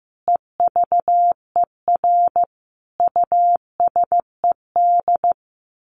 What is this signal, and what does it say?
Morse code, heard as a keyed tone, about 15 wpm, 700 Hz. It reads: EVER USED
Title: Sentences from Top 1100 Words 15wpm